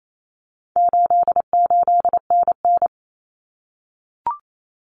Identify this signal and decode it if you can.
Morse code, heard as a keyed tone, about 28 wpm, 700 Hz. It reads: 88NN
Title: Common Cut Numbers - 4-Digit 28wpm